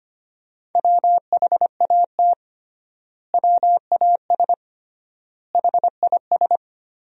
Morse code, heard as a keyed tone, about 25 wpm, 700 Hz. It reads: WHAT WAS HIS
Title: 3x Word-Spacing - Sentences from Top 200 Words - 25wpm